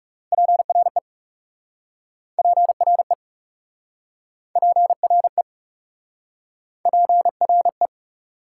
Morse code, heard as a keyed tone, about 37 wpm, 700 Hz. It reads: PRE PRE PRE PRE
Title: Speed-Racing Prefixes and Suffixes 30wpm